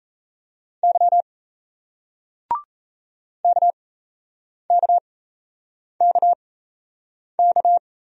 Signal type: Morse code

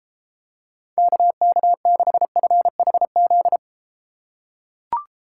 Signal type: Morse code

